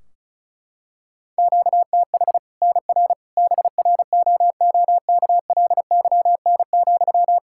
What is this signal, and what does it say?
Morse code, heard as a keyed tone, about 36 wpm, 700 Hz. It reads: QTH NR BROOKLYN,
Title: Practice QSO - Exchange 1 - Beginning - 35wpm